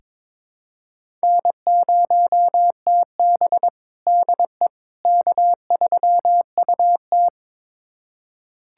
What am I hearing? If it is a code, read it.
Morse code, heard as a keyed tone, about 22 wpm, 700 Hz. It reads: N0TB DE K3UT
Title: USA Callsign DE Callsign - 4 Characters - 22wpm